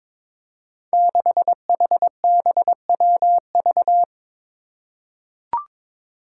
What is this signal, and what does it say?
Morse code, heard as a keyed tone, about 21 wpm, 700 Hz. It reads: 6HBWV
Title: Five Challenging Letter-Number Combinations 22wpm